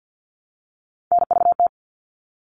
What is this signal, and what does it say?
Morse code, heard as a keyed tone, about 50 wpm, 700 Hz. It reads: NVT
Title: All Cut Numbers - 3-Digit 50wpm